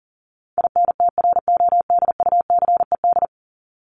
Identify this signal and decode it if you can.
Morse code, heard as a keyed tone, about 40 wpm, 700 Hz. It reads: INTRODUCED
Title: ED Suffix - Top 100 Words with Base Words 40wpm